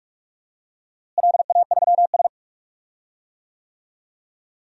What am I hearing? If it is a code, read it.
Morse code, heard as a keyed tone, about 46 wpm, 700 Hz. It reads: LA3S
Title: DXCC-100 Call Signs - 4 Characters - Repeated 2x - 45wpm